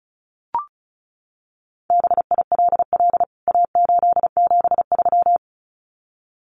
Morse code, heard as a keyed tone, about 34 wpm, 700 Hz. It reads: BILL A873
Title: CWT Exchange 35wpm